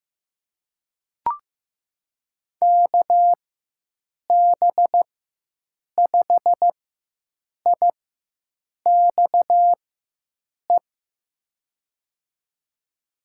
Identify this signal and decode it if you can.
Morse code, heard as a keyed tone, about 15 wpm, spaced as for 7 wpm, 700 Hz. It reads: KB5IXE
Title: USA Call Signs - 6 Characters - 4x Character Spacing - 15wpm